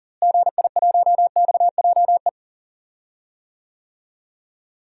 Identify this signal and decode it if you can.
Morse code, heard as a keyed tone, about 38 wpm, 700 Hz. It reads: GI1XJE
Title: DXCC-100 Call Signs - 6 Characters - Repeated 2x - 40wpm